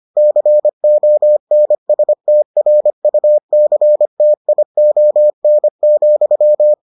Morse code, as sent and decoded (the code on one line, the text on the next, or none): -.-. --- -. ... - .-. ..- -.-. - .. --- -. --..--
CONSTRUCTION,